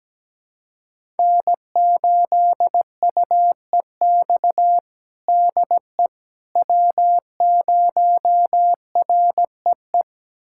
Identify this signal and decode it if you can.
Morse code, heard as a keyed tone, about 17 wpm, 700 Hz. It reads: N8UEX DE W0REE
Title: USA Callsign DE Callsign - 5 Characters - 17wpm